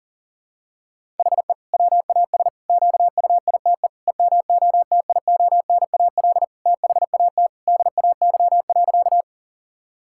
Morse code, heard as a keyed tone, about 40 wpm, 700 Hz. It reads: HE WAS QUITE EMOTIONAL THAT DAY.
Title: Sentences from Top 1000 Words - Review Part 5 - 40wpm